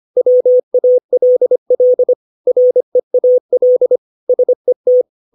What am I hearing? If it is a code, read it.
Morse code, heard as a keyed tone, about 25 wpm, 500 Hz. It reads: WALL REAL SET
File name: 632 words / 632 Words 25WPM